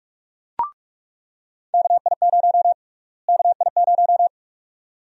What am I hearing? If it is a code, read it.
Morse code, heard as a keyed tone, about 46 wpm, 700 Hz. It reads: KI0 KI0
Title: USA Call Sign Prefixes - Repeated 2x - 45wpm